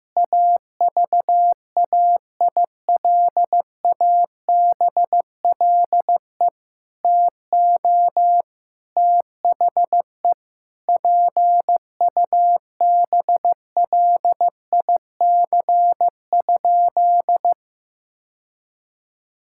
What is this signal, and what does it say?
Morse code, heard as a keyed tone, about 15 wpm, 700 Hz. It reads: AVAILABLE TO THE PUBLIC?
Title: Sentences from Top 1000 Words - Review Part 5 - 15wpm